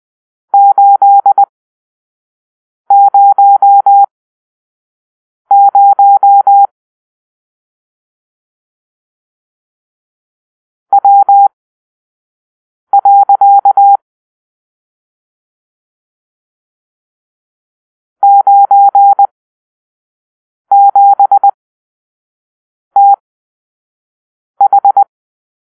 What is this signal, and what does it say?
Morse code, heard as a keyed tone, about 20 wpm, spaced as for 5 wpm, 800 Hz. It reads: 800 W. 97TH
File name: arrl / Grp-7 Drill-3